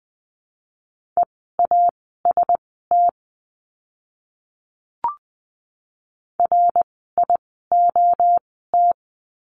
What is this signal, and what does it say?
Morse code, heard as a keyed tone, about 20 wpm, spaced as for 15 wpm, 700 Hz. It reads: EAST RIOT
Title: 061 Lesson 11f - Letter h - Words 20wpm